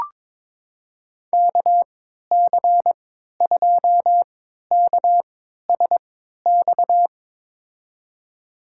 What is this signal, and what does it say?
Morse code, heard as a keyed tone, about 22 wpm, spaced as for 13 wpm, 700 Hz. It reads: KC2KSX
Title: USA Call Signs - 3x Character Spacing - 22wpm